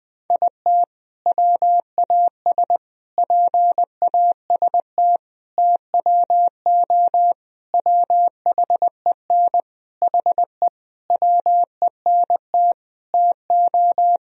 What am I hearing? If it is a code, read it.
Morse code, heard as a keyed tone, about 20 wpm, 700 Hz. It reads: IT WAS PAST TWO WHEN HE WENT TO
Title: Sentences from Top 1000 Words - Review Part 3 - 20wpm